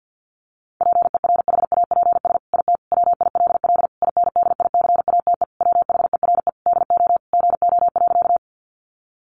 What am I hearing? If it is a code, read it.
Morse code, heard as a keyed tone, about 50 wpm, 700 Hz. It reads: PERHAPS IT WILL INDICATE WHERE DO GO.
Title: Sentences from Top 1300 Words 50wpm